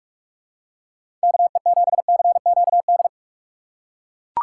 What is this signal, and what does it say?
Morse code, heard as a keyed tone, about 41 wpm, 700 Hz. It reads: KE7CQD
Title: USA Call Signs - 6 Characters - 6x Character Spacing - 45wpm